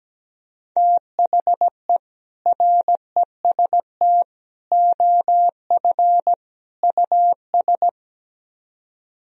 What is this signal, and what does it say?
Morse code, heard as a keyed tone, about 17 wpm, 700 Hz. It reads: THE REST OF US
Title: Speed-Racing - Sets of 4 Words 17wpm